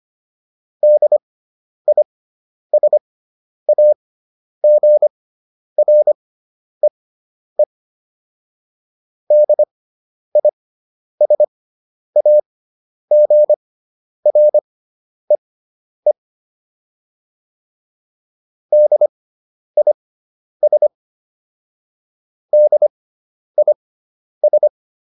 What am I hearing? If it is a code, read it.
Morse code, heard as a keyed tone, about 25 wpm, spaced as for 10 wpm, 600 Hz. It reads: DISAGREE DISAGREE DIS DIS